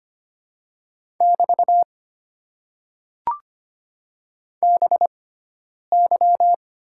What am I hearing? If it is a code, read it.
Morse code, heard as a keyed tone, about 25 wpm, spaced as for 9 wpm, 690 Hz. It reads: <BT> BY
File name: Top 90 QSO Elements - 6x Character Spacing - 25wpm